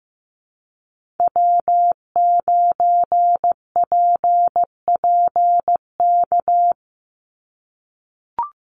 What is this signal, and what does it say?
Morse code, heard as a keyed tone, about 15 wpm, 700 Hz. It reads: W9PPK
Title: USA Call Signs 15wpm